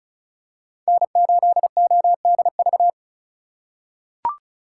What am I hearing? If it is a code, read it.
Morse code, heard as a keyed tone, about 36 wpm, 700 Hz. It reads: N8ODV
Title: USA Call Signs - 2x Character Spacing - 35wpm